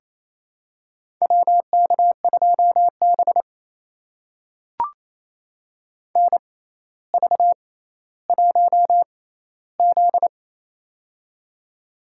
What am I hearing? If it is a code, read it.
Morse code, heard as a keyed tone, about 28 wpm, spaced as for 14 wpm, 700 Hz. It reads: WK2B NV1Z
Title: USA Call Signs - 4 Characters - 6x Character Spacing - 28wpm